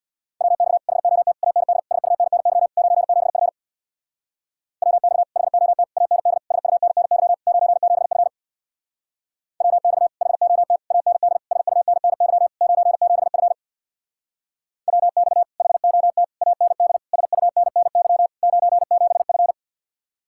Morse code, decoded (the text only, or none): WX HOT AND SUNNY 97F WX HOT AND SUNNY 97F WX HOT AND SUNNY 97F WX HOT AND SUNNY 97F